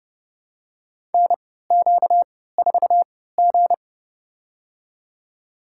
Morse code, heard as a keyed tone, about 30 wpm, spaced as for 17 wpm, 700 Hz. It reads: NQ4G
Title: USA Call Signs - 4 Characters - 3x Character Spacing - 30wpm